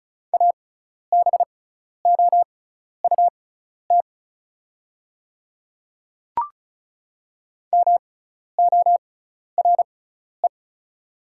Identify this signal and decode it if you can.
Morse code, heard as a keyed tone, about 36 wpm, spaced as for 12 wpm, 700 Hz. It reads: ABOUT MORE
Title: 6x Character Spacing - Top 100 Words - 35wpm